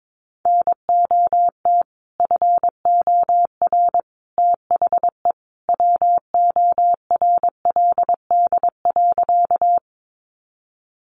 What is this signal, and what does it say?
Morse code, heard as a keyed tone, about 22 wpm, 700 Hz. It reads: NOT FOR THE WORLD.
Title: Sentences from Top 600 Words 22wpm